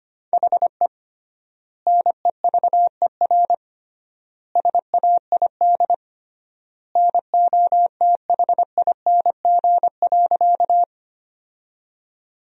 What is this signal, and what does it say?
Morse code, heard as a keyed tone, about 25 wpm, 700 Hz. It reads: HE NEVER SAID NOTHING.
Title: 3x Word-Spacing - Sentences from Top 500 Words - 25wpm